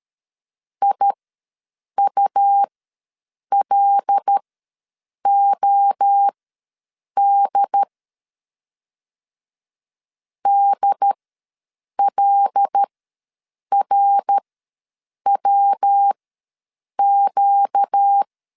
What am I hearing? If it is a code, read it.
Morse code, heard as a keyed tone, about 13 wpm, spaced as for 7 wpm, 800 Hz. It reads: IULOD DLRWQ
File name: LEZIONI / 5 LBVC / 5 L ws 12wpm 2958